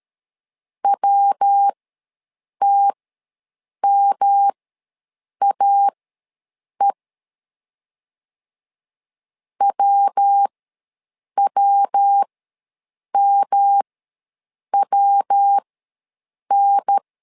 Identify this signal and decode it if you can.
Morse code, heard as a keyed tone, about 13 wpm, spaced as for 7 wpm, 800 Hz. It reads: WTMAE WWMWN
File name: LEZIONI / 3 MWRO / 3 MW ws 12wpm 1816